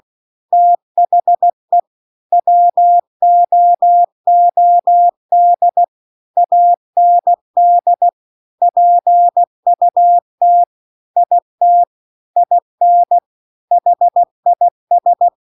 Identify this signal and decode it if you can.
Morse code, heard as a keyed tone, about 16 wpm, 700 Hz. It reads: THE WOOD AND PUT IT IN HIS
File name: CW-The-Big-Twoheaded-River-2-16WPM